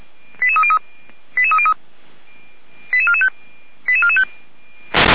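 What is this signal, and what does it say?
Five-tone selective call (ZVEI), heard as quick runs of five tones, tones of about 70 ms. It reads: 88303 88303 88445 88445
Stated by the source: ZVEI five-tone call received by SDR.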